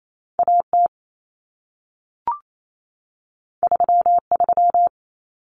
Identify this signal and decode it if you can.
Morse code, heard as a keyed tone, about 28 wpm, 700 Hz. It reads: AT 33
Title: Common Cut Numbers - 2-Digit 28wpm